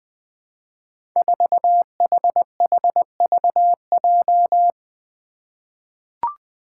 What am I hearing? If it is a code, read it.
Morse code, heard as a keyed tone, about 21 wpm, 700 Hz. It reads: 4HHVJ
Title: Five Challenging Letter-Number Combinations 20wpm